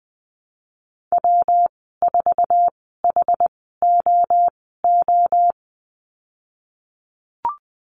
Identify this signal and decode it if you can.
Morse code, heard as a keyed tone, about 20 wpm, spaced as for 15 wpm, 700 Hz. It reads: W4HOO
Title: 122 Lesson 20g - Number 3 - Call Signs 20wpm